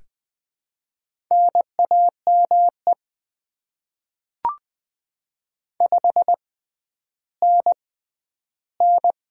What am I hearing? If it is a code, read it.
Morse code, heard as a keyed tone, about 20 wpm, spaced as for 11 wpm, 700 Hz. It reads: NAME 5NN
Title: Top 20 QSO Elements - 6x Character Spacing - 20wpm